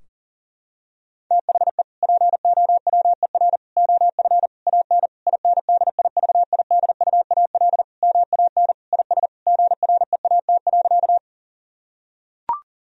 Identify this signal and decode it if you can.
Morse code, heard as a keyed tone, about 40 wpm, 700 Hz. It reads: THE POWER OF AN INDIVIDUAL MAN IS GREAT.
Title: Sentences from Top 1000 Words - Review Part 4 - 40wpm